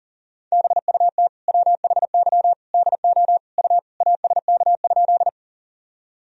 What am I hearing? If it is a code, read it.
Morse code, heard as a keyed tone, about 40 wpm, 700 Hz. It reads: BUT WHY DO U ASK?
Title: Sentences from Top 200 Words and Abbreviations 40wpm